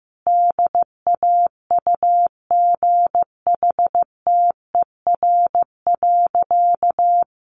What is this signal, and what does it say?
Morse code, heard as a keyed tone, about 15 wpm, 700 Hz. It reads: DAUGHTER.
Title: Sentences from Top 1500 Words 15wpm